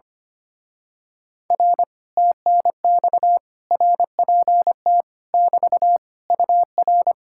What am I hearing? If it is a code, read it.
Morse code, heard as a keyed tone, about 24 wpm, 700 Hz. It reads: R TNX RPT <BT> UR
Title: Speed-Racing Common QSO Phrases Encore - 25wpm